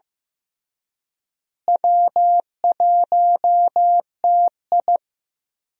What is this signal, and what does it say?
Morse code, heard as a keyed tone, about 15 wpm, 700 Hz. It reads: W1TI
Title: USA Call Signs - 4 Characters - Repeated 2x - 15wpm